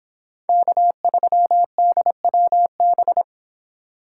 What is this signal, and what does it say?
Morse code, heard as a keyed tone, about 26 wpm, 700 Hz. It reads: K3DWB